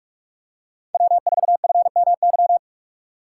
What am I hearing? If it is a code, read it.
Morse code, heard as a keyed tone, about 40 wpm, 700 Hz. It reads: W4FMY